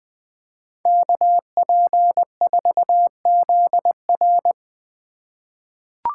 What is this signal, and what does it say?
Morse code, heard as a keyed tone, about 20 wpm, 700 Hz. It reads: KP4ZR
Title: USA Call Signs - 8x Character Spacing - 20wpm